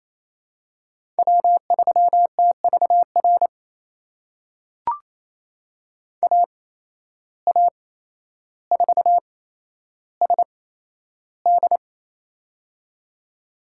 Morse code, heard as a keyed tone, about 28 wpm, spaced as for 12 wpm, 700 Hz. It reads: W3TVR AA4SD
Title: USA Call Signs - 5 Characters - 8x Character Spacing - 28wpm